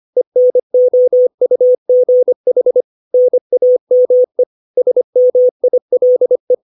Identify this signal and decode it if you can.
Morse code, heard as a keyed tone, about 25 wpm, 500 Hz. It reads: ENOUGH NAME SMILE
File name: 632 words / 632 Words 25WPM